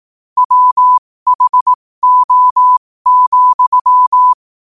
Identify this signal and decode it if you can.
Morse code, heard as a keyed tone, about 18 wpm, spaced as for 15 wpm, 1000 Hz. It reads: WHO,
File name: qotdfiles / 15wpm